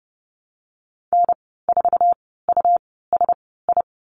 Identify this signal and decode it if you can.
Morse code, heard as a keyed tone, about 30 wpm, spaced as for 17 wpm, 700 Hz. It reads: N4USI
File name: Lesson 19g - Letter w - Call Signs 30wpm